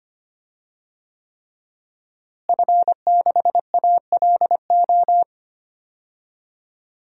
Morse code, heard as a keyed tone, about 25 wpm, 700 Hz. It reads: F6ALO
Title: DXCC-100 Call Signs - 5 Characters - Repeated 3x - 25wpm